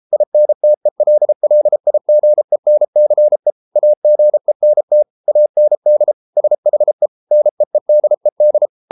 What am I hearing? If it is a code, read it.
Morse code, heard as a keyed tone, about 33 wpm, 600 Hz. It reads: INTELLIGENCE AGENT AND SHE NEEDED